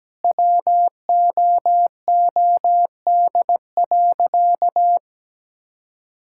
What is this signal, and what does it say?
Morse code, heard as a keyed tone, about 17 wpm, 700 Hz. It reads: WOOD.
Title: Sentences from Top 1000 Words - Review Part 6 - 17wpm